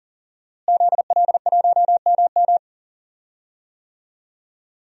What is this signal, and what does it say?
Morse code, heard as a keyed tone, about 40 wpm, 700 Hz. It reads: ZL1MM
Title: DXCC-100 Call Signs - 5 Characters - Repeated 3x - 40wpm